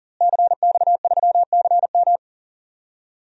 Morse code, heard as a keyed tone, about 37 wpm, 700 Hz. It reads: CX3CM
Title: DXCC-100 Call Signs - 5 Characters - Repeated 3x - 40wpm